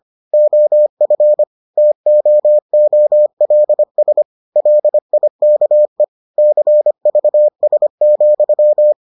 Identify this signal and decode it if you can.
Morse code, heard as a keyed tone, about 25 wpm, 600 Hz. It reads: OF TOOLS LIKE CVS,